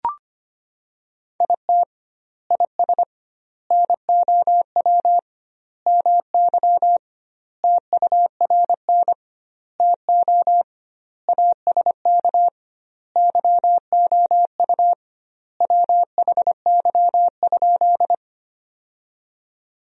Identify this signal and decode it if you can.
Morse code, heard as a keyed tone, about 25 wpm, 700 Hz. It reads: IT IS NOW MY TURN TO ASK YOU WHY?
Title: 2x Word-Spacing - Sentences from Top 200 Words - 25wpm